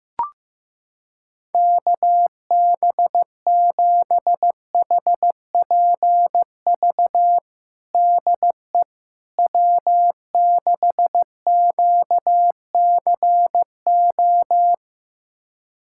Morse code, heard as a keyed tone, about 15 wpm, 700 Hz. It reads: KB7HPV DE W6QCO
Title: USA Callsign DE Callsign 15wpm